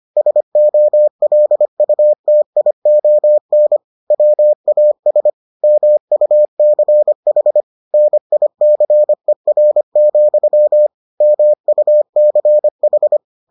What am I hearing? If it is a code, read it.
Morse code, heard as a keyed tone, about 25 wpm, 600 Hz. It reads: SOLUTION WAS MUCH NICER, MUCH